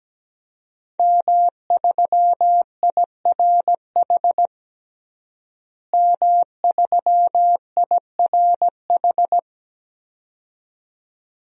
Morse code, heard as a keyed tone, about 17 wpm, 700 Hz. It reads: M3IRH M3IRH
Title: DXCC-100 Call Signs - 5 Characters - Repeated 2x - 17wpm